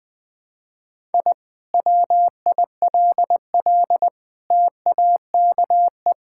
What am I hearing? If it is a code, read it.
Morse code, heard as a keyed tone, about 20 wpm, 700 Hz. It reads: I WILL TAKE
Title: Sentences from Top 1000 Words - Review Part 6 - 20wpm